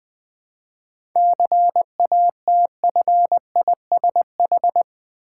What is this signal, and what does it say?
Morse code, heard as a keyed tone, about 20 wpm, 690 Hz. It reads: CATFISH
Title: Difficult Words with Hints - 20wpm